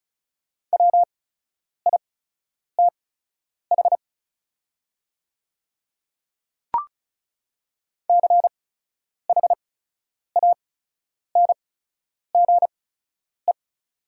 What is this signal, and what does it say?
Morse code, heard as a keyed tone, about 34 wpm, spaced as for 10 wpm, 690 Hz. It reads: WITH CHANGE
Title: 8x Character Spacing - Top 200 Words - 35wpm